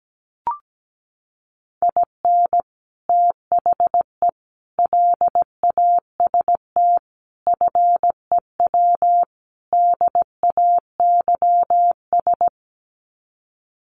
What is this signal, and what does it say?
Morse code, heard as a keyed tone, about 17 wpm, 700 Hz. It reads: IN THE LAST FEW DAYS
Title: Sets of 5 Words Encore2 17wpm